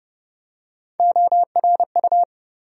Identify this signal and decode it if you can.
Morse code, heard as a keyed tone, about 30 wpm, 700 Hz. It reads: ORU
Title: USA Call Sign Postfixes - 3 Letters 30wpm